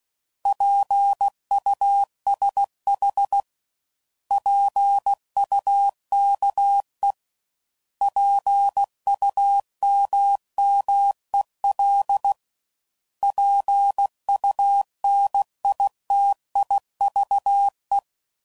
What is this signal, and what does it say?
Morse code, heard as a keyed tone, about 16 wpm, 800 Hz. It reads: PUSH PUKE PUMMEL PUNITIVE
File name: Prefixes / By First Letter / Ps